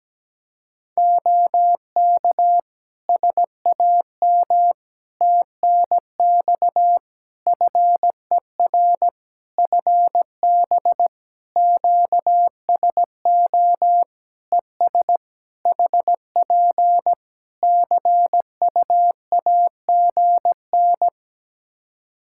Morse code, heard as a keyed tone, about 17 wpm, 700 Hz. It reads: OK SAM TNX FER FB QSO ES HP CUAGN
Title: Practice QSO - Exchange 4 - Ending - 17wpm